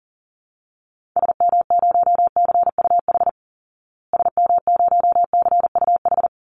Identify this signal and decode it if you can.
Morse code, heard as a keyed tone, about 40 wpm, 700 Hz. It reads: SM0CUH SM0CUH
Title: Call Signs Repeated 3x - 40wpm